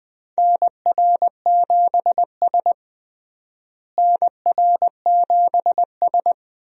Morse code, heard as a keyed tone, about 20 wpm, 700 Hz. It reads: NR7S NR7S
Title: USA Call Signs - 4 Characters - Repeated 3x - 20wpm